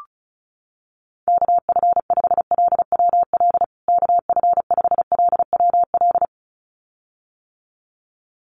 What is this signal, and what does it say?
Morse code, heard as a keyed tone, about 34 wpm, 700 Hz. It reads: KF5LWL KF5LWL
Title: USA Call Signs Repeated 2x 35wpm